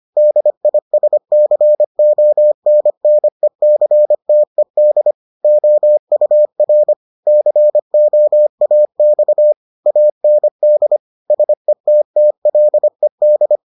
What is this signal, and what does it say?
Morse code, heard as a keyed tone, about 25 wpm, 600 Hz. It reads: DISCONNECTED OUR COAX AND SETTLED